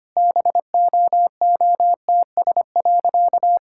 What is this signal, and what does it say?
Morse code, heard as a keyed tone, about 25 wpm, 690 Hz. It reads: BOOTS.
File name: Easy Sentences with Hints - 25wpm